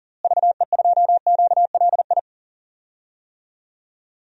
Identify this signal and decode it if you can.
Morse code, heard as a keyed tone, about 40 wpm, 700 Hz. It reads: VE2QLI
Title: DXCC-100 Call Signs - 6 Characters - Repeated 3x - 40wpm